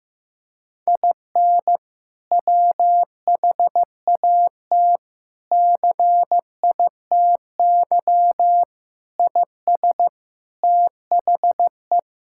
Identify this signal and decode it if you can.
Morse code, heard as a keyed tone, about 15 wpm, 700 Hz. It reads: IN WHAT CITY IS THE
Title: Sentences from Top 1000 Words - Review Part 6 - 15wpm